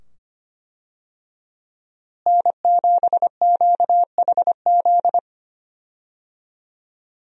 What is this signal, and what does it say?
Morse code, heard as a keyed tone, about 25 wpm, 700 Hz. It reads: N7QHZ
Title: USA Call Signs - 5 Characters - Repeated 2x - 25wpm